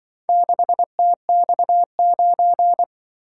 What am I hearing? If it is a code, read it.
Morse code, heard as a keyed tone, about 24 wpm, 700 Hz. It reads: 6TX9
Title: Speed-Racing - Four Letter-Number Combinations - 20wpm